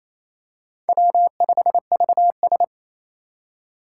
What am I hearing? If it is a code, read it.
Morse code, heard as a keyed tone, about 27 wpm, 700 Hz. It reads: W5VS
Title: USA Call Signs - 4 Characters - 3x Character Spacing - 28wpm